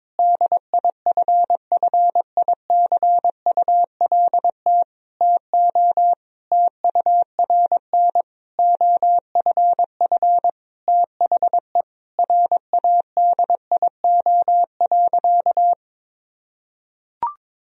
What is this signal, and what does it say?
Morse code, heard as a keyed tone, about 22 wpm, 700 Hz. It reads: DIFFICULT TO TURN OFF THE RADIO.
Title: Sentences from Top 1000 Words - Review Part 3 - 22wpm